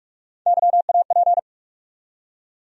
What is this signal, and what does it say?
Morse code, heard as a keyed tone, about 42 wpm, 700 Hz. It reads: YAP
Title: USA Call Sign Postfixes - 3 Letters 45wpm